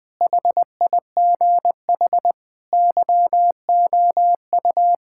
Morse code, heard as a keyed tone, about 20 wpm, 700 Hz. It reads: HIGH YOU
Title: Sentences from Top 1000 Words - Review Part 6 - 20wpm